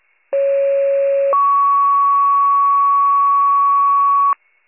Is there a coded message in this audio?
Yes: two-tone page from 0.3 to 4.3 s, heard as a one-second tone, then a three-second tone, about 550 then 1120 Hz.